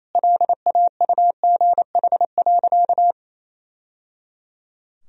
Morse code, heard as a keyed tone, about 28 wpm, 700 Hz. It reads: LAUGH.
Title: Sentences from Top 1000 Words - Review Part 2 - 28wpm